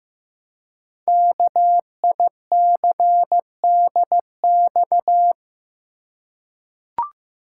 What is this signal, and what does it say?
Morse code, heard as a keyed tone, about 15 wpm, 700 Hz. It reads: KICDX
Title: Code Groups - 5 Letter:Numbers - 15wpm